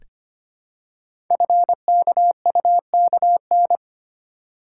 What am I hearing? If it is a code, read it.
Morse code, heard as a keyed tone, about 25 wpm, 700 Hz. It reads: FKUKN
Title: Code Groups - 5 Letters - 25wpm